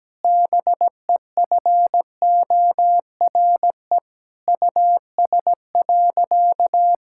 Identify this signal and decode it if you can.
Morse code, heard as a keyed tone, about 17 wpm, 700 Hz. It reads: BEFORE US.
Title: Sentences from Top 1000 Words - Review Part 5 - 17wpm